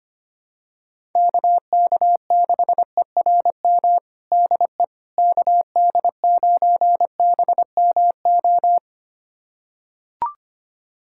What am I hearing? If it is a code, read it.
Morse code, heard as a keyed tone, about 25 wpm, 700 Hz. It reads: KK6ERM DE KD9BMO
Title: USA Callsign DE Callsign - 6 Characters - 25wpm